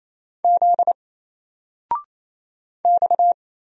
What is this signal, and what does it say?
Morse code, heard as a keyed tone, about 27 wpm, 700 Hz. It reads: Z X
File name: Single-Letter-Number-28wpm